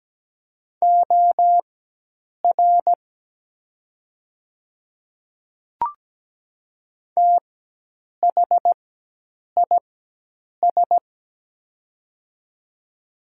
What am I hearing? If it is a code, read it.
Morse code, heard as a keyed tone, about 17 wpm, spaced as for 8 wpm, 700 Hz. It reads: OR THIS
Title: 4x Character Spacing - Top 100 Words - 17wpm